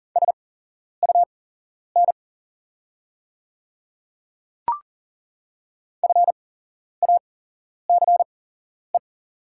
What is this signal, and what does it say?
Morse code, heard as a keyed tone, about 40 wpm, spaced as for 11 wpm, 700 Hz. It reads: SUN FACE
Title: 8x Character Spacing - Top 300 Words - 40wpm